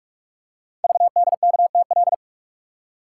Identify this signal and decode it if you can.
Morse code, heard as a keyed tone, about 44 wpm, 700 Hz. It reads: VDKTL
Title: Code Groups - 5 Letters - 45wpm